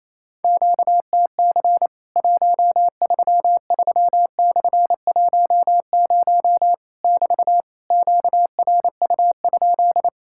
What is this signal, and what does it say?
Morse code, heard as a keyed tone, about 28 wpm, 700 Hz. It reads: QTC 133/10 <BT> QRU?
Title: Worked All Europe DX Contest - QTC Exchange - 28wpm